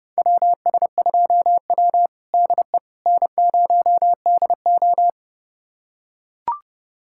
Morse code, heard as a keyed tone, about 32 wpm, 700 Hz. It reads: WS2W DE N0DO
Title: USA Callsign DE Callsign - 4 Characters - 30wpm